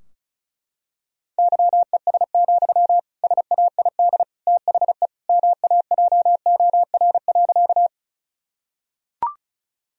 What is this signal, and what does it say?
Morse code, heard as a keyed tone, about 34 wpm, 700 Hz. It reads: YES, SAID THE MAJOR.
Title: Sentences from Top 1000 Words - Review Part 4 - 35wpm